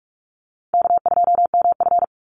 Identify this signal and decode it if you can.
Morse code, heard as a keyed tone, about 44 wpm, 700 Hz. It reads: K2MF
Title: Easy Call Signs Repeated 2x - 45wpm